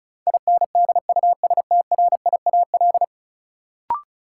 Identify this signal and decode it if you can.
Morse code, heard as a keyed tone, about 34 wpm, 700 Hz. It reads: INDUSTRIAL
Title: Sets of 5 Words - New Word Drill - 35wpm